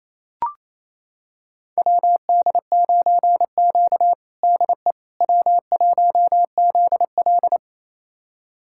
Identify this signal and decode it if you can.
Morse code, heard as a keyed tone, about 28 wpm, 700 Hz. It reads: WD9Q DE W1ZL
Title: USA Callsign DE Callsign - 4 Characters - 28wpm